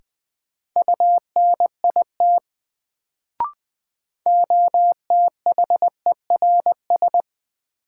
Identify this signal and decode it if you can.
Morse code, heard as a keyed tone, about 20 wpm, 700 Hz. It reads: UNIT OTHERS
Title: Sentences from Top 1000 Words - Review Part 2 - New Word Drill - 20wpm